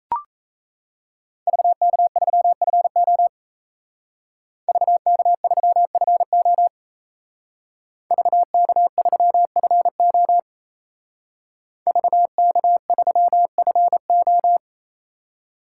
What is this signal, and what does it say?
Morse code, heard as a keyed tone, about 34 wpm, 700 Hz. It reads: VK3FO VK3FO VK3FO VK3FO
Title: Speed Racing - USA 2x2 Call Signs 28wpm